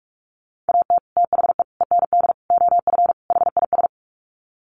Morse code, heard as a keyed tone, about 44 wpm, 700 Hz. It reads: AT THE END OF HIS
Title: Sets of 5 Words Encore 45wpm